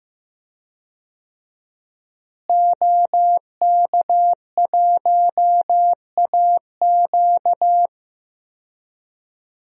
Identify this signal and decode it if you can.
Morse code, heard as a keyed tone, about 15 wpm, 700 Hz. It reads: OK1AQ
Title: DXCC-100 Call Signs - 5 Characters - Repeated 2x - 15wpm